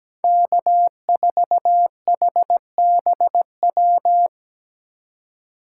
Morse code, heard as a keyed tone, about 17 wpm, 700 Hz. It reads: K4HBW
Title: USA Call Signs - 5 Characters - 2x Character Spacing - 17wpm